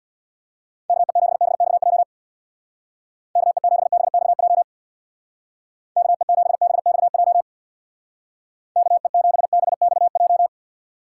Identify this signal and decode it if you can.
Morse code, heard as a keyed tone, about 64 wpm, 700 Hz. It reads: KE7BXJ KE7BXJ KE7BXJ KE7BXJ